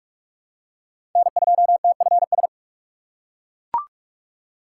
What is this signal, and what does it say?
Morse code, heard as a keyed tone, about 41 wpm, 700 Hz. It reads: N2TFS
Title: USA Call Signs - 6x Character Spacing - 45wpm